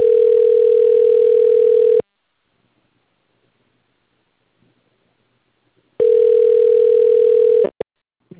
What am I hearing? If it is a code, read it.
Ringback tone (North America), 440 + 480 Hz.